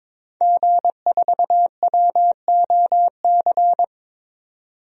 Morse code, heard as a keyed tone, about 22 wpm, 700 Hz. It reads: G4WOC
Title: DXCC-100 Call Signs - 5 Characters - 22wpm